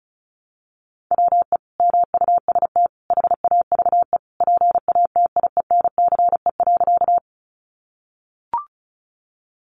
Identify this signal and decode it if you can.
Morse code, heard as a keyed tone, about 34 wpm, 700 Hz. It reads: WE MUST HAVE PATIENCE.
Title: Sentences from Top 1300 Words 35wpm